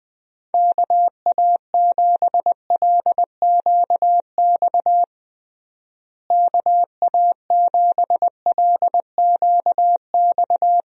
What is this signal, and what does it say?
Morse code, heard as a keyed tone, about 21 wpm, 700 Hz. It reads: KA7LQX KA7LQX
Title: USA Call Signs - 6 Characters - Repeated 2x - 20wpm